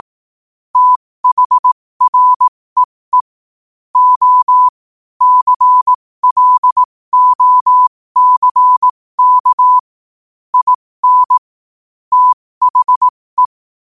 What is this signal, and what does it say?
Morse code, heard as a keyed tone, about 18 wpm, spaced as for 15 wpm, 1000 Hz. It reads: THREE O CLOCK IN THE
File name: qotdfiles / 15wpm